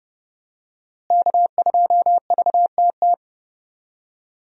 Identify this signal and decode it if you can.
Morse code, heard as a keyed tone, about 32 wpm, 700 Hz. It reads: K2VTT
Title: USA Call Signs - 5 Characters - Repeated 2x - 30wpm